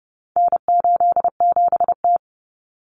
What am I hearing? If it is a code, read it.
Morse code, heard as a keyed tone, about 30 wpm, 700 Hz. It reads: N87T
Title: Common Cut Numbers - 4-Digit 30wpm